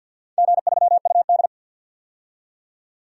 Morse code, heard as a keyed tone, about 50 wpm, 700 Hz. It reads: G3UD
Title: DXCC-100 Call Signs - 4 Characters - 50wpm